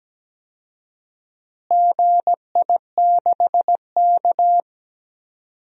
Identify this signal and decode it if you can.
Morse code, heard as a keyed tone, about 17 wpm, 700 Hz. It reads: GI6K
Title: DXCC-100 Call Signs - 4 Characters - Repeated 2x - 17wpm